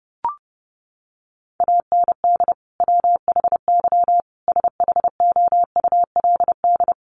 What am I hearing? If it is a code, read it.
Morse code, heard as a keyed tone, about 30 wpm, 700 Hz. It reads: AND WHY SHOULD
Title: Sentences from Top 300 Words 30wpm